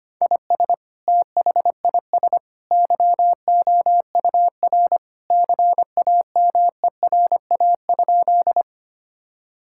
Morse code, heard as a keyed tone, about 25 wpm, 700 Hz. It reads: IS THIS YOUR CAMERA?
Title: Sentences from Top 1000 Words - Review Part 5 - 25wpm